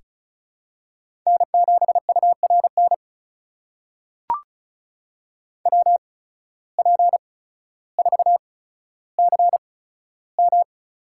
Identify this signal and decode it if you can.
Morse code, heard as a keyed tone, about 36 wpm, spaced as for 15 wpm, 700 Hz. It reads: N7URN WP4CM
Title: USA Call Signs - 5 Characters - 8x Character Spacing - 35wpm